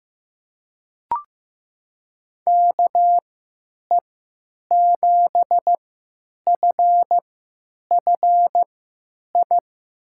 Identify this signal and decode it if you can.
Morse code, heard as a keyed tone, about 15 wpm, spaced as for 9 wpm, 700 Hz. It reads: KE7FFI
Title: USA Call Signs - 6 Characters - 3x Character Spacing - 15wpm